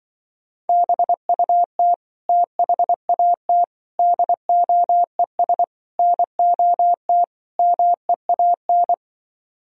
Morse code, heard as a keyed tone, about 24 wpm, 700 Hz. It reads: BUT THAT DOES NOT MEAN
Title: Speed-Racing - Sets of 5 Words 20wpm